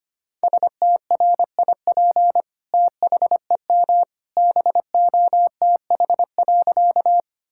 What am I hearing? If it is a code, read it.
Morse code, heard as a keyed tone, about 25 wpm, 700 Hz. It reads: STRIP THEM BOTH.
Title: Sentences from Top 1600 Words 25wpm